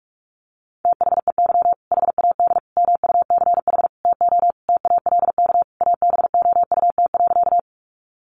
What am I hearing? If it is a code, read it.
Morse code, heard as a keyed tone, about 46 wpm, 700 Hz. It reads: THEY HAD MUCH TO TALK ABOUT.
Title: Sentences from Top 400 Words 45wpm